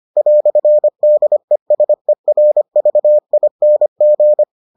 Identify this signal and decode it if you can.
Morse code, heard as a keyed tone, about 25 wpm, 600 Hz. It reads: "DESERVING